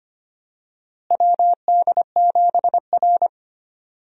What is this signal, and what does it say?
Morse code, heard as a keyed tone, about 25 wpm, 700 Hz. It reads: WD7R
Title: USA Call Signs - 4 Characters - Repeated 3x - 25wpm